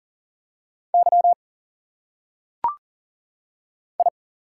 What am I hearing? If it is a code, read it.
Morse code, heard as a keyed tone, about 40 wpm, 700 Hz. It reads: Y I
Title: Single Letters - Phonetic Pronunciation 40wpm